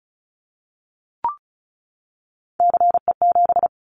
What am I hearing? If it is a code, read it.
Morse code, heard as a keyed tone, about 34 wpm, 700 Hz. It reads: CE7
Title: Callsign Prefixes for Top 100 DXCC - 35wpm